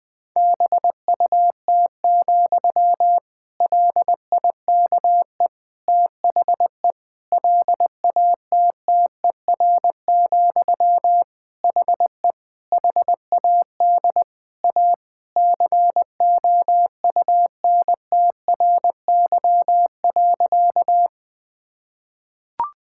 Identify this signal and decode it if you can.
Morse code, heard as a keyed tone, about 20 wpm, 700 Hz. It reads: BUT, LIKE THE LATTER, HE HAD A COUNTRY.
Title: Sentences from Top 1600 Words 20wpm